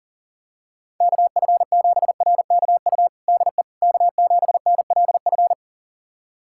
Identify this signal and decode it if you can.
Morse code, heard as a keyed tone, about 40 wpm, 700 Hz. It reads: KF7RKU DE K7NLF
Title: USA Callsign DE Callsign 40wpm